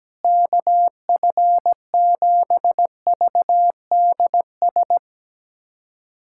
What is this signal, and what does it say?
Morse code, heard as a keyed tone, about 17 wpm, 700 Hz. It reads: KF7VDS
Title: USA Callsign DE Callsign - 6 Characters - 17wpm